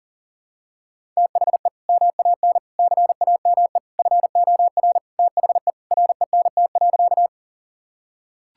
Morse code, heard as a keyed tone, about 40 wpm, 700 Hz. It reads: THE MAN CAME FOR THE RENT.
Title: Sentences from Top 1000 Words - Review Part 5 - 40wpm